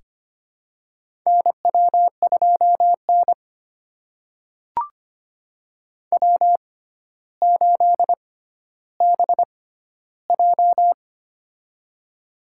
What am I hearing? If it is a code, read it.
Morse code, heard as a keyed tone, about 25 wpm, spaced as for 13 wpm, 700 Hz. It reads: NW2N W8BJ
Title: USA Call Signs - 4 Characters - 6x Character Spacing - 25wpm